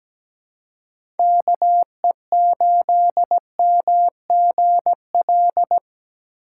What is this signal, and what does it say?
Morse code, heard as a keyed tone, about 17 wpm, 700 Hz. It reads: KE8MGL
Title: USA Call Signs - 3x Character Spacing - 17wpm